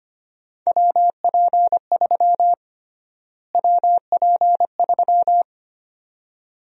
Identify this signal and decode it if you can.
Morse code, heard as a keyed tone, about 25 wpm, 700 Hz. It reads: WP3 WP3
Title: USA Call Sign Prefixes - Repeated 2x - 25wpm